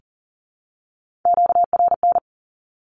Morse code, heard as a keyed tone, about 40 wpm, 700 Hz. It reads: QRN
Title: Q-Codes Repeated 3x 40wpm